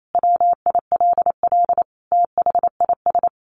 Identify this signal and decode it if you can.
Morse code, heard as a keyed tone, about 28 wpm, 700 Hz. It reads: WILL THIS
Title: Sentences from Top 1400 Words 28wpm